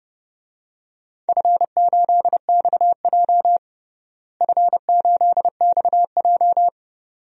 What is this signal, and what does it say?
Morse code, heard as a keyed tone, about 30 wpm, 700 Hz. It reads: F8XJ F8XJ
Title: DXCC-100 Call Signs - 4 Characters - Repeated 2x - 30wpm